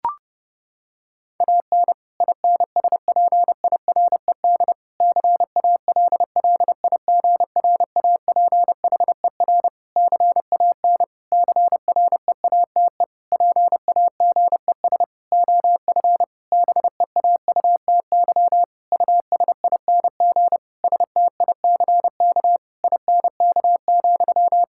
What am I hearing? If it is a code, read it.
Morse code, heard as a keyed tone, about 30 wpm, 700 Hz. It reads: AN INSPIRED CALLIGRAPHER CAN CREATE PAGES OF BEAUTY USING STICK INK,
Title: Pangram Challenge v2 - 30wpm